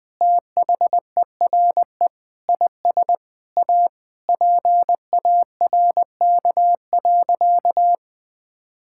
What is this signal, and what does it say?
Morse code, heard as a keyed tone, about 20 wpm, 700 Hz. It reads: THERE IS A PARK.
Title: Sentences with 5-Word Phrases - 20wpm